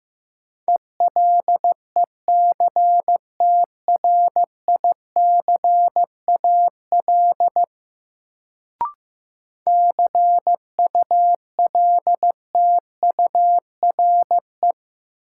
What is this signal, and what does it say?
Morse code, heard as a keyed tone, about 15 wpm, 700 Hz. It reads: ELECTRICAL CULTURE
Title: Sentences from Top 1000 Words - Review Part 6 - New Word Drill - 15wpm